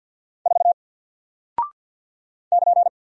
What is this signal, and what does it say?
Morse code, heard as a keyed tone, about 49 wpm, 700 Hz. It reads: 4 <KN>
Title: Single Character (Letter:Number:Punctuation:Prosign) Alt 50wpm